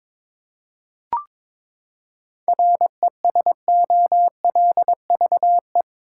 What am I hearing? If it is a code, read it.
Morse code, heard as a keyed tone, about 22 wpm, 700 Hz. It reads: RESOLVE
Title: Seven Letter Words - 22wpm